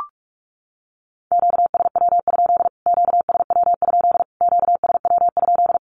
Morse code, heard as a keyed tone, about 44 wpm, 700 Hz. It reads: QSW? QSW? QSW?
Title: Q-Codes Repeated 3x 45wpm